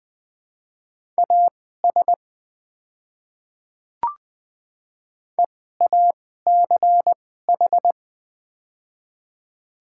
Morse code, heard as a keyed tone, about 20 wpm, spaced as for 14 wpm, 700 Hz. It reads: AS EACH
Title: 2x Character Spacing - Top 200 Words - 20wpm